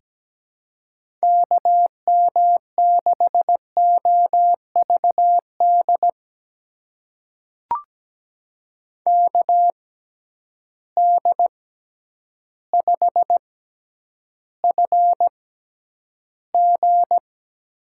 Morse code, heard as a keyed tone, about 17 wpm, spaced as for 9 wpm, 700 Hz. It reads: KM6OVD KD5FG
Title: USA Call Signs - 6 Characters - 6x Character Spacing - 17wpm